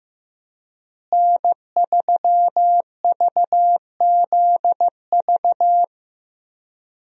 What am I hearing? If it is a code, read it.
Morse code, heard as a keyed tone, about 15 wpm, 700 Hz. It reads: N3VZV
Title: USA Call Signs - 5 Characters - Repeated 2x - 15wpm